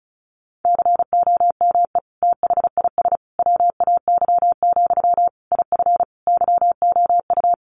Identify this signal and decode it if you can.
Morse code, heard as a keyed tone, about 34 wpm, 700 Hz. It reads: COME THIS WAY, IF YOU
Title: Sentences from Top 1500 Words 35wpm